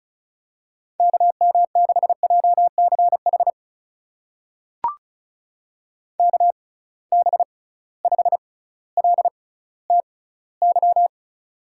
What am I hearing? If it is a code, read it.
Morse code, heard as a keyed tone, about 36 wpm, spaced as for 18 wpm, 700 Hz. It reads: KM6JCH KB5LTY
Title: USA Call Signs - 6 Characters - 6x Character Spacing - 35wpm